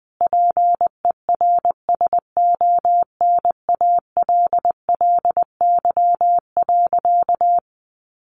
Morse code, heard as a keyed tone, about 20 wpm, 700 Hz. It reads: PERSONALLY.
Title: Sentences from Top 1400 Words 20wpm